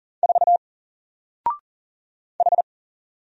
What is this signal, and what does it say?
Morse code, heard as a keyed tone, about 40 wpm, 700 Hz. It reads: 4 H
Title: Single Letter-Number 40wpm